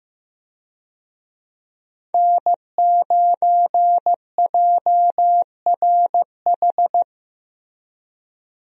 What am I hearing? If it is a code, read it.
Morse code, heard as a keyed tone, about 15 wpm, 700 Hz. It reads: N9JRH
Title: USA Call Signs - 5 Characters - Repeated 2x - 15wpm